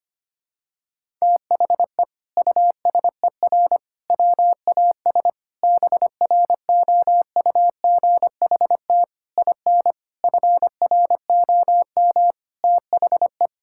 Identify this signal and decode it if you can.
Morse code, heard as a keyed tone, about 25 wpm, 700 Hz. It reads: THE USER WAS BROUGHT IN FROM THE
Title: Sentences from Top 1000 Words - Review Part 5 - 25wpm